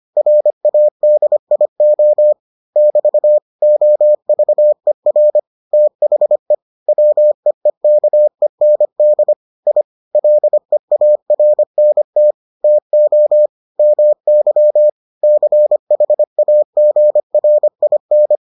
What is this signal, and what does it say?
Morse code, heard as a keyed tone, about 25 wpm, 600 Hz. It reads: RADIO <BT> OVER THE WEEKEND I LEARNT TO MY CHAGRIN